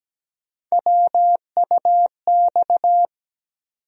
Morse code, heard as a keyed tone, about 17 wpm, 700 Hz. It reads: WUX
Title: USA Call Sign Postfixes - 3 Letters 17wpm